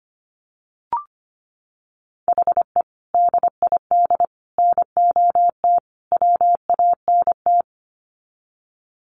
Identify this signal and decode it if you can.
Morse code, heard as a keyed tone, about 25 wpm, 700 Hz. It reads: HE DID NOT WANT
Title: Sets of 4 Words 25wpm